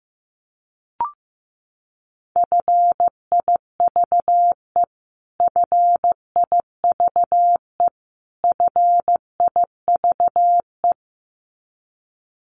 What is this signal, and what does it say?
Morse code, heard as a keyed tone, about 15 wpm, 700 Hz. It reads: FIVE FIVE FIVE
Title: Top 500 Words Repeated 15wpm